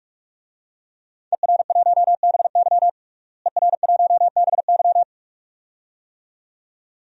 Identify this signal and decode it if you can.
Morse code, heard as a keyed tone, about 45 wpm, 700 Hz. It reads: ER1BY ER1BY
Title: DXCC-100 Call Signs - 5 Characters - Repeated 2x - 45wpm